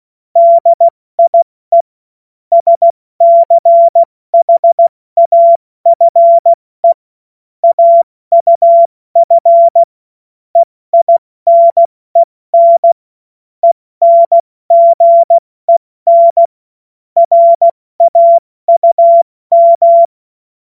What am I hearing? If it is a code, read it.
Morse code, heard as a keyed tone, about 16 wpm, 680 Hz. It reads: DIE SCHAFE AUF EINEN ENGEN RAUM
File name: Morsekurs / Witze / Witz 26 Dezember 2023 70 F80